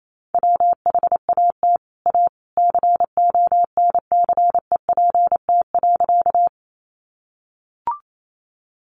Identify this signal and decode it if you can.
Morse code, heard as a keyed tone, about 28 wpm, 700 Hz. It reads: WHAT A CONCEPT.
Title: Sentences from Top 1100 Words 28wpm